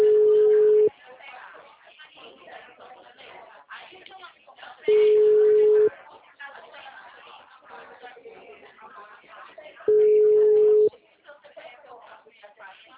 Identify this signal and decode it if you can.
Ringback tone (Europe, 425 Hz), with speech.
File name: out- -2027- - - .37635